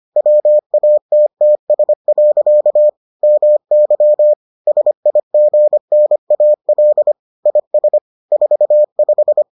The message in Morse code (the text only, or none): WATTS. MY SIGNAL IS 45